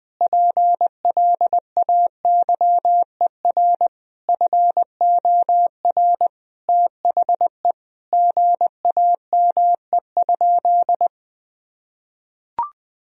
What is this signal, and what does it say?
Morse code, heard as a keyed tone, about 20 wpm, 700 Hz. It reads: PLAYER FOR THE GAME?
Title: Sentences from Top 1000 Words - Review Part 4 - 20wpm